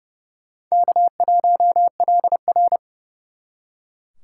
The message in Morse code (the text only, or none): K1LR